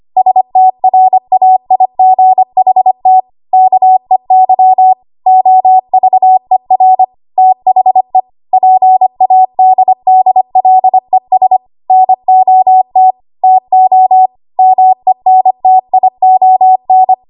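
Morse code, heard as a keyed tone, about 25 wpm, 750 Hz. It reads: STRAIGHT KEY OVER THE PADDLES NOT TO MENTION
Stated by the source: ARRL 25 WPM code practice archive.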